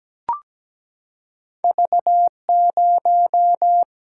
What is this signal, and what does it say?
Morse code, heard as a keyed tone, about 17 wpm, 700 Hz. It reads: V0
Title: Single-Letter Single-Number 17wpm